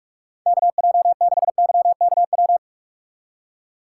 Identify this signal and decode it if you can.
Morse code, heard as a keyed tone, about 46 wpm, 700 Hz. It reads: KJ6YKW
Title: USA Call Signs - 6 Characters - 4x Character Spacing - 45wpm